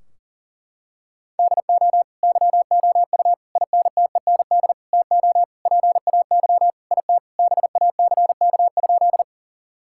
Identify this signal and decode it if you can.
Morse code, heard as a keyed tone, about 40 wpm, 700 Hz. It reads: DO YOU INTEND TO PAY IT BACK?
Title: Sentences from Top 1600 Words 40wpm